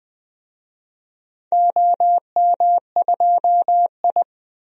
Morse code, heard as a keyed tone, about 20 wpm, 700 Hz. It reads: OM2I
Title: DXCC-100 Call Signs - 4 Characters - 20wpm